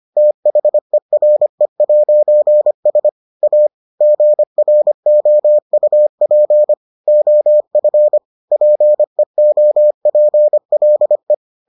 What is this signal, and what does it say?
Morse code, heard as a keyed tone, about 25 wpm, 600 Hz. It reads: THERE'S A GROUP OF PEOPLE